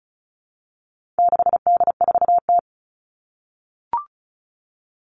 Morse code, heard as a keyed tone, about 34 wpm, 700 Hz. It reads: 6D4T
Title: All Cut Numbers - 4-Digit 35wpm